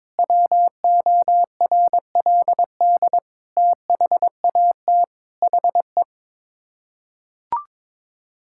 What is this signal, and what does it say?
Morse code, heard as a keyed tone, about 22 wpm, 700 Hz. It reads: WORLD THAT HE
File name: practice / 22wpm / Sets of 8 Words - 22wpm